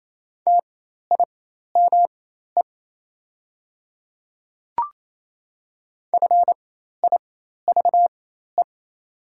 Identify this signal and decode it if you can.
Morse code, heard as a keyed tone, about 28 wpm, spaced as for 13 wpm, 690 Hz. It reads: TIME FIVE
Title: 4x Character Spacing - Top 500 Words - 28wpm